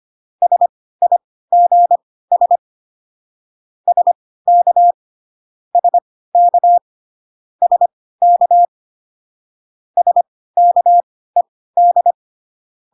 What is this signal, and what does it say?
Morse code, heard as a keyed tone, about 25 wpm, spaced as for 16 wpm, 700 Hz. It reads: SIGS SK SK SK SKED